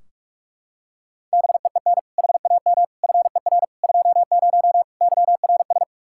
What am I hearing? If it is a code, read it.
Morse code, heard as a keyed tone, about 45 wpm, 700 Hz. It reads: BEEN HAM FER 20 YRS
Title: K1USN SST QSO Practice 45wpm